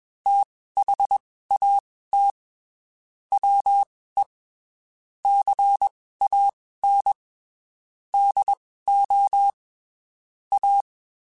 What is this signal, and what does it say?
Morse code, heard as a keyed tone, about 21 wpm, spaced as for 15 wpm, 800 Hz. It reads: THAT WE CAN DO A
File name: (spaced) / QSOs Spaced-mp3s / WD0FBT @20 sp